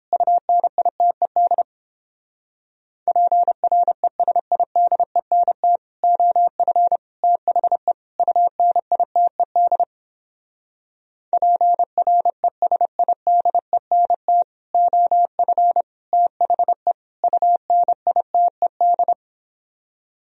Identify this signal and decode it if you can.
Morse code, heard as a keyed tone, about 31 wpm, 700 Hz. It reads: UNITED PRESIDENT OF THE UNITED PRESIDENT OF THE UNITED